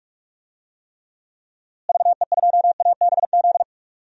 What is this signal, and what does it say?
Morse code, heard as a keyed tone, about 46 wpm, 700 Hz. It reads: VE2ABZ
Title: DXCC-100 Call Signs - 6 Characters - Repeated 2x - 45wpm